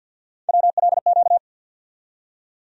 Morse code, heard as a keyed tone, about 46 wpm, 700 Hz. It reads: WLQ
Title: USA Call Sign Postfixes - 3 Letters 50wpm